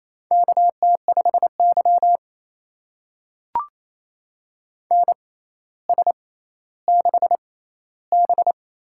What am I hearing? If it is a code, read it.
Morse code, heard as a keyed tone, about 29 wpm, spaced as for 15 wpm, 700 Hz. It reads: KT5Y NS6B
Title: USA Call Signs - 4 Characters - 6x Character Spacing - 28wpm